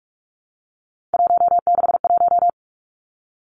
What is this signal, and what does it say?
Morse code, heard as a keyed tone, about 46 wpm, 700 Hz. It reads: 161
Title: Three Challenging Letter-Number Combinations 45wpm